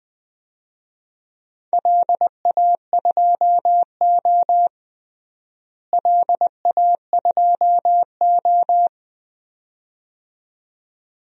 Morse code, heard as a keyed tone, about 20 wpm, 700 Hz. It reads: LA2O LA2O
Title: DXCC-100 Call Signs - 4 Characters - Repeated 2x - 20wpm